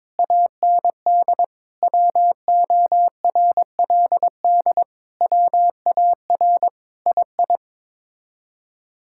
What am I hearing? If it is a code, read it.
Morse code, heard as a keyed tone, about 22 wpm, 700 Hz. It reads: AND WORLD WAR II
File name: practice / 22wpm / Sets of 7 Words - 22wpm